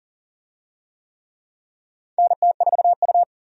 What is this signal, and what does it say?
Morse code, heard as a keyed tone, about 40 wpm, 700 Hz. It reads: NT4U
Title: USA Call Signs - 4 Characters - 40wpm